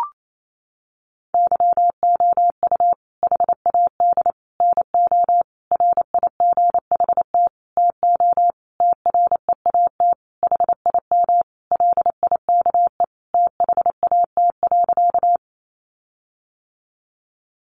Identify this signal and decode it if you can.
Morse code, heard as a keyed tone, about 28 wpm, 700 Hz. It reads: YOU HAD NO RIGHT TO TREAT HIM LIKE THAT.
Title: Sentences from Top 1300 Words 28wpm